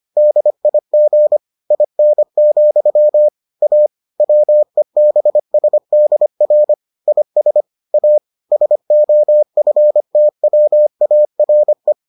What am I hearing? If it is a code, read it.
Morse code, heard as a keyed tone, about 25 wpm, 600 Hz. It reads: DIG IN, A WEBSDR IS A SOFTWARE